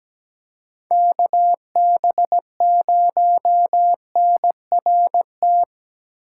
Morse code, heard as a keyed tone, about 17 wpm, 700 Hz. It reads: KB0NRT
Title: USA Call Signs - 6 Characters - 4x Character Spacing - 17wpm